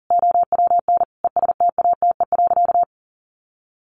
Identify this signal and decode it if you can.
Morse code, heard as a keyed tone, about 40 wpm, 700 Hz. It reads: OWN ESTATE.
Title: Sentences from Top 1200 Words 40wpm